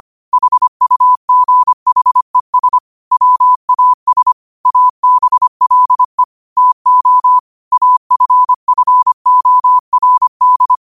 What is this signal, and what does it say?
Morse code, heard as a keyed tone, about 25 wpm, 1000 Hz. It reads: HUGHES WAS ABLE TO AFFORD
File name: qotdfiles / 25wpm